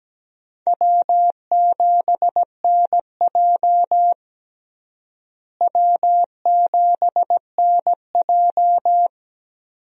Morse code, heard as a keyed tone, about 17 wpm, 700 Hz. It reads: W7NJ W7NJ
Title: USA Call Signs - 4 Characters - Repeated 2x - 17wpm